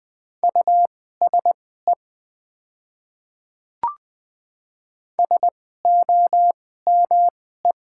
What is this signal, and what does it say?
Morse code, heard as a keyed tone, about 20 wpm, spaced as for 14 wpm, 700 Hz. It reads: USE SOME
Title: 2x Character Spacing - Top 100 Words - 20wpm